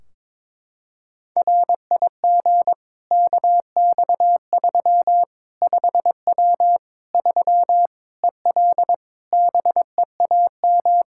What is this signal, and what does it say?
Morse code, heard as a keyed tone, about 22 wpm, 700 Hz. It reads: RIG KX3 5W 3 EL BEAM
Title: K1USN SST QSO Practice 22wpm